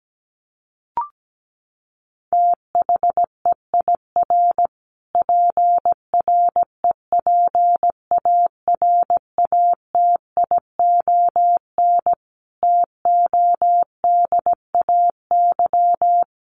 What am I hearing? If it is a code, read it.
Morse code, heard as a keyed tone, about 17 wpm, 700 Hz. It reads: THEIR PREPARATION TODAY
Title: Sentences from Top 1400 Words 17wpm